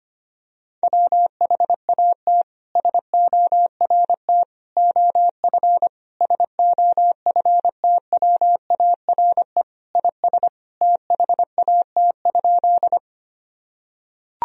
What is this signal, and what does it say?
Morse code, heard as a keyed tone, about 25 wpm, 700 Hz. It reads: WHAT SORT OF SOFTWARE IS THAT?
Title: Sentences from Top 1000 Words - Review Part 3 - 25wpm